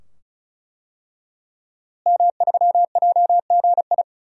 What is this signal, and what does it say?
Morse code, heard as a keyed tone, about 34 wpm, 700 Hz. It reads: M3JGI
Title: DXCC-100 Call Signs - 5 Characters - Repeated 2x - 35wpm